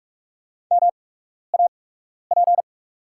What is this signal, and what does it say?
Morse code, heard as a keyed tone, about 44 wpm, spaced as for 12 wpm, 690 Hz. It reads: MAP
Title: 8x Character Spacing - Top 500 Words - 45wpm